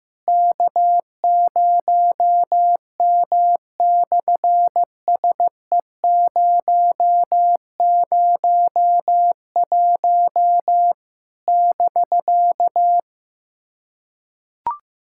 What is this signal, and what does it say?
Morse code, heard as a keyed tone, about 15 wpm, 700 Hz. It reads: K0M/SE001 <BK>
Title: SOTA QSO Practice 15wpm